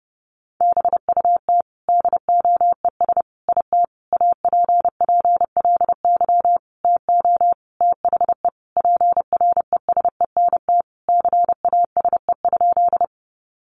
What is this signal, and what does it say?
Morse code, heard as a keyed tone, about 30 wpm, 700 Hz. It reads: BUT DOES IT APPLY TO THE PRESENT CASE?
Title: Sentences from Top 1200 Words 30wpm